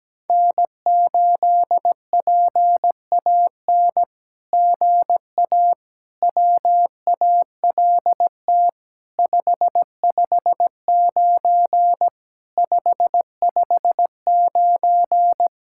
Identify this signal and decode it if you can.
Morse code, heard as a keyed tone, about 17 wpm, 700 Hz. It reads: N8PAN GA WALT 559 559
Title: SOTA QSO Practice 17wpm